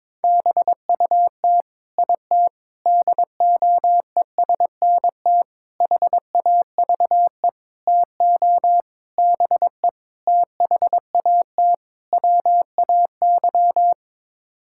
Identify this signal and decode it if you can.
Morse code, heard as a keyed tone, about 22 wpm, 700 Hz. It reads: BUT IT DOESNT HAVE TO BE THAT WAY
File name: practice / 22wpm / Sets of 8 Words - 22wpm